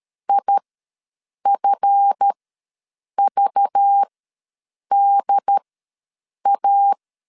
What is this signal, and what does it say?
Morse code, heard as a keyed tone, about 13 wpm, spaced as for 7 wpm, 800 Hz. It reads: IFVDA
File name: LEZIONI / 7 GF9Z / 7 GF9 ws 12wpm 3344